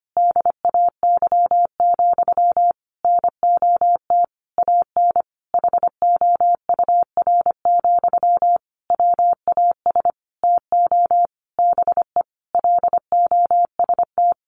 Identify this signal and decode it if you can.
Morse code, heard as a keyed tone, about 25 wpm, 700 Hz. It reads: DAY, NOT AN HOUR, WAS TO BE LOST
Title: Sentences from Top 1400 Words 25wpm